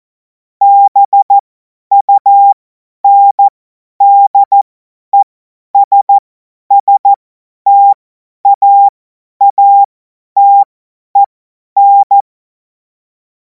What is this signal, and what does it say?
Morse code, heard as a keyed tone, about 14 wpm, spaced as for 10 wpm, 800 Hz. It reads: BUNDESSTAATEN
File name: Morsekurs / Witze / Witz 12. Mai 2026 50 F70